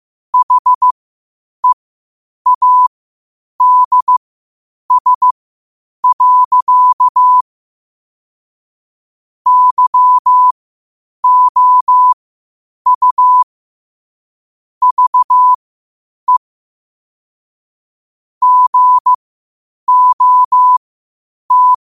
Morse code, heard as a keyed tone, about 15 wpm, spaced as for 8 wpm, 1000 Hz. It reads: HEADS. YOU VE GOT
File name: qotdfiles / 8wpm